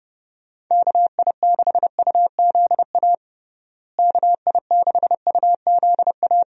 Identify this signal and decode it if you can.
Morse code, heard as a keyed tone, about 30 wpm, 700 Hz. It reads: KI6UZA KI6UZA
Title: USA Call Signs - 6 Characters - Repeated 2x - 30wpm